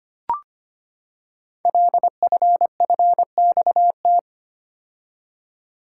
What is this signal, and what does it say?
Morse code, heard as a keyed tone, about 25 wpm, 700 Hz. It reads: LFFXT
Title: Code Groups - 5 Letters - 25wpm